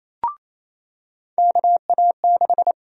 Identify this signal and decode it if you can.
Morse code, heard as a keyed tone, about 28 wpm, 700 Hz. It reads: KA6
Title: USA Call Sign Prefixes - 28wpm